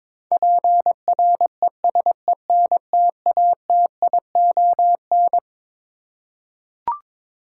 Morse code, heard as a keyed tone, about 22 wpm, 700 Hz. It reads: PRESENTATION
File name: Twelve Letter Words - 22wpm